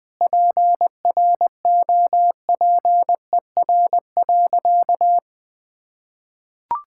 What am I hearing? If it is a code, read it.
Morse code, heard as a keyed tone, about 20 wpm, 700 Hz. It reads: PROPER.
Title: Sentences from Top 1000 Words - Review Part 5 - 20wpm